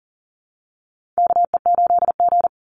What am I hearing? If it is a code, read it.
Morse code, heard as a keyed tone, about 40 wpm, 700 Hz. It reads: KE8G
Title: Call Signs Repeated 3x - 40wpm